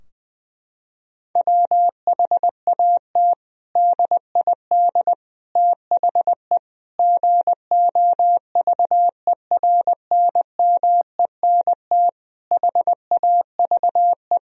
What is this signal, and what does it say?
Morse code, heard as a keyed tone, about 20 wpm, 700 Hz. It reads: WHAT DID THE GOVERNMENT HAVE
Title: Sentences from Top 1000 Words - Review Part 3 - 20wpm